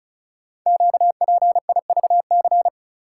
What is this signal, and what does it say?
Morse code, heard as a keyed tone, about 36 wpm, 700 Hz. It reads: QPIVC
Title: Code Groups - 5 Letters - 35wpm